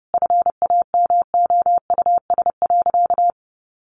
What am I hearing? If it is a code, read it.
Morse code, heard as a keyed tone, about 30 wpm, 700 Hz. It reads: FAMOUS.
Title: Sentences from Top 1500 Words 30wpm